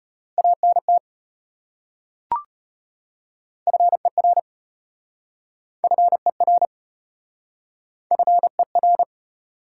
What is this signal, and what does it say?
Morse code, heard as a keyed tone, about 33 wpm, 700 Hz. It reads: ANT FER FER FER